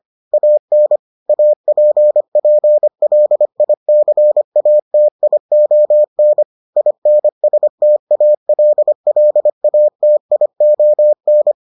Morse code, heard as a keyed tone, about 25 wpm, 600 Hz. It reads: AN APPLICATION INSTALLATION